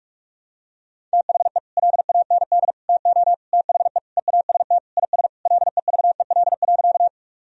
Morse code, heard as a keyed tone, about 46 wpm, 700 Hz. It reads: THE LAND TO THE EAST IS LEVEL.
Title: Sentences from Top 1000 Words - Review Part 4 - 45wpm